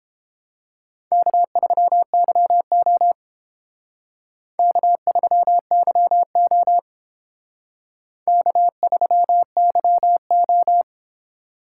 Morse code, heard as a keyed tone, about 30 wpm, 700 Hz. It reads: K3YO K3YO K3YO